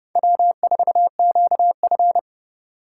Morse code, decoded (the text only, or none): W4QF